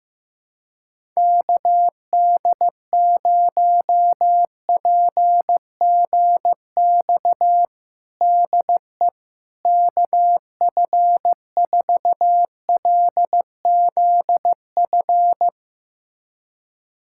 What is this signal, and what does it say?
Morse code, heard as a keyed tone, about 15 wpm, 700 Hz. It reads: KD0PGX DE KF4LZF
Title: USA Callsign DE Callsign 15wpm